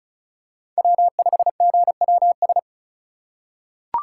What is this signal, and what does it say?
Morse code, heard as a keyed tone, about 36 wpm, 700 Hz. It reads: W5GWS
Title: USA Call Signs - 5 Characters - 4x Character Spacing - 35wpm